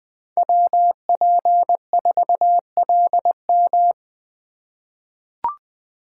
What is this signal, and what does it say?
Morse code, heard as a keyed tone, about 20 wpm, 700 Hz. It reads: WP4LM
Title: USA Call Signs - 5 Characters - 3x Character Spacing - 20wpm